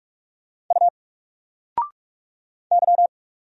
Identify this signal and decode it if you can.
Morse code, heard as a keyed tone, about 44 wpm, 700 Hz. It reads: U Y
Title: Single Character (Letter:Number:Punctuation:Prosign) 45wpm